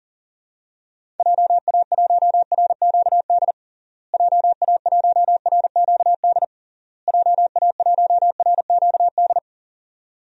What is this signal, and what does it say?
Morse code, heard as a keyed tone, about 40 wpm, 700 Hz. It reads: JA1RQD JA1RQD JA1RQD
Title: DXCC-100 Call Signs - 6 Characters - Repeated 3x - 40wpm